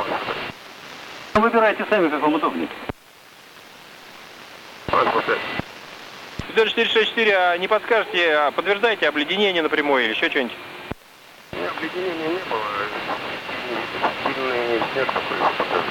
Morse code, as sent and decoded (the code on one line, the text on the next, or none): none